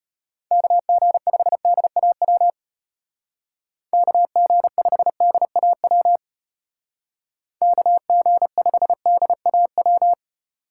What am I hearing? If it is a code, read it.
Morse code, heard as a keyed tone, about 33 wpm, 700 Hz. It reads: KG5DAW KG5DAW KG5DAW